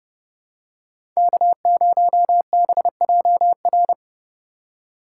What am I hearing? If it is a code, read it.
Morse code, heard as a keyed tone, about 30 wpm, 700 Hz. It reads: K0BJR
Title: USA Call Signs - 5 Characters - 3x Character Spacing - 30wpm